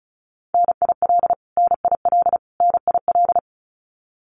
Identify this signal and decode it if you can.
Morse code, heard as a keyed tone, about 34 wpm, 700 Hz. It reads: NIL NIL NIL
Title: Abbreviations Repeated 3x - 35wpm